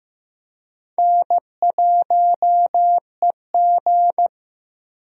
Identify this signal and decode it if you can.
Morse code, heard as a keyed tone, about 15 wpm, 700 Hz. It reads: N1EG
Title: USA Call Signs - 4 Characters - Repeated 3x - 15wpm